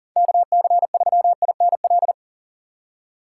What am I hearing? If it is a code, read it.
Morse code, heard as a keyed tone, about 40 wpm, 700 Hz. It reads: KC3INL
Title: USA Call Signs - 4x Character Spacing - 40wpm